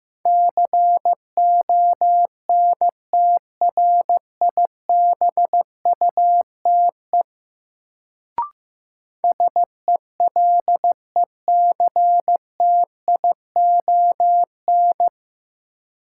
Morse code, heard as a keyed tone, about 15 wpm, 700 Hz. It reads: CONTRIBUTE SELECTION
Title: Sentences from Top 1400 Words - New Word Drill Alt - 15wpm